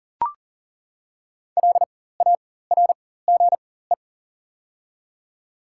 Morse code, heard as a keyed tone, about 42 wpm, spaced as for 19 wpm, 700 Hz. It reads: LARGE
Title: 4x Character Spacing - Top 200 Words - 40wpm